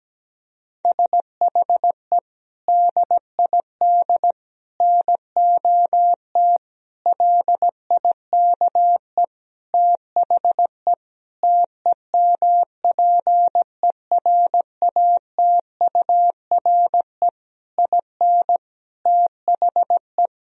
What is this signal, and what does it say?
Morse code, heard as a keyed tone, about 17 wpm, 700 Hz. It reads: SHE DID NOT LIKE THE TEMPERATURE IN THE
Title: Sentences from Top 1000 Words - Review Part 4 - 17wpm